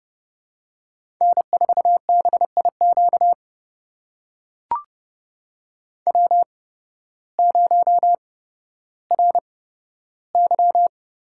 Morse code, heard as a keyed tone, about 30 wpm, spaced as for 14 wpm, 700 Hz. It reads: N4BIQ W0RY
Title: USA Call Signs - 8x Character Spacing - 30wpm